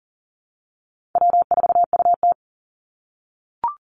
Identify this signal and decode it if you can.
Morse code, heard as a keyed tone, about 40 wpm, 700 Hz. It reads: W4UT
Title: Call Signs - 40wpm